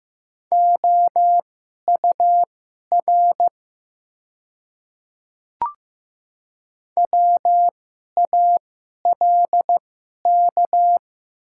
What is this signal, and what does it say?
Morse code, heard as a keyed tone, about 15 wpm, spaced as for 11 wpm, 700 Hz. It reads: OUR WALK
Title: 2x Character Spacing - Top 300 Words - 15wpm